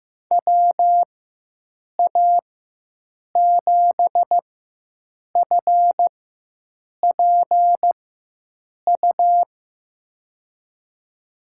Morse code, heard as a keyed tone, about 15 wpm, spaced as for 7 wpm, 700 Hz. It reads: WA7FPU
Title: USA Call Signs - 6 Characters - 4x Character Spacing - 15wpm